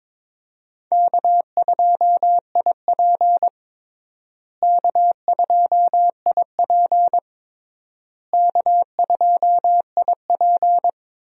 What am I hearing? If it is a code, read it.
Morse code, heard as a keyed tone, about 22 wpm, 700 Hz. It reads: K2IP K2IP K2IP
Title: USA Call Signs - 4 Characters - Repeated 3x - 22wpm